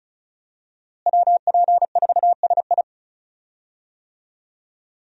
Morse code, heard as a keyed tone, about 34 wpm, 700 Hz. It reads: WP4SI
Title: USA Call Signs - 5 Characters - Repeated 3x - 35wpm